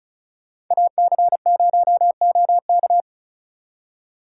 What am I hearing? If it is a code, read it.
Morse code, heard as a keyed tone, about 44 wpm, 700 Hz. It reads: AC0OK